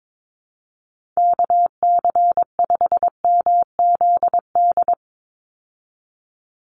Morse code, heard as a keyed tone, about 22 wpm, 700 Hz. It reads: KC5MZD
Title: USA Call Signs 22wpm